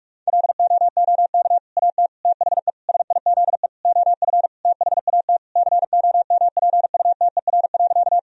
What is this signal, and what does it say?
Morse code, heard as a keyed tone, about 47 wpm, 700 Hz. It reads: LOOK AT THE SIZE OF THAT COMPUTER.
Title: Sentences from Top 1000 Words - Review Part 3 - 45wpm